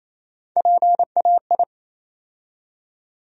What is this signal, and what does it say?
Morse code, heard as a keyed tone, about 28 wpm, 700 Hz. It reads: PAI
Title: USA Call Sign Postfixes - 3 Letters 28wpm